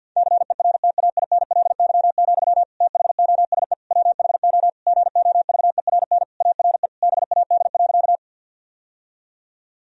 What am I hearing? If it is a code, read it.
Morse code, heard as a keyed tone, about 51 wpm, 700 Hz. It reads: CERTAINLY, THOSE WHO GOVERN ARE BAD.
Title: Sentences from Top 1000 Words - Review Part 2 - 50wpm